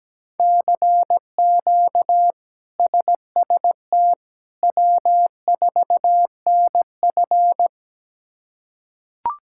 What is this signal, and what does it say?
Morse code, heard as a keyed tone, about 17 wpm, 700 Hz. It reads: CQ SST W4NF
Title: K1USN SST QSO Practice - 17wpm